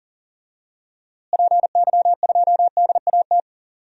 Morse code, heard as a keyed tone, about 40 wpm, 700 Hz. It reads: PY2DAT
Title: DXCC-100 Call Signs - 6 Characters - Repeated 3x - 40wpm